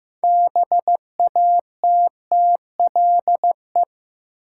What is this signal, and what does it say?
Morse code, heard as a keyed tone, about 16 wpm, 700 Hz. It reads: BATTLE
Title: Six Letter Words - 15wpm